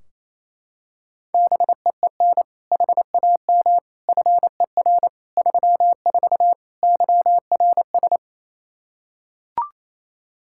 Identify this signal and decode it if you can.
Morse code, heard as a keyed tone, about 28 wpm, 700 Hz. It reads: BEEN HAM FER 34 YRS
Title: Practice QSO - Exchange 3 - More Info - 28wpm